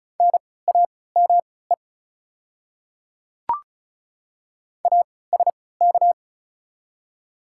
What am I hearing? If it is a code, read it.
Morse code, heard as a keyed tone, about 36 wpm, spaced as for 20 wpm, 690 Hz. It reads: NAME ASK
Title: 3x Character Spacing - Top 200 Words - 35wpm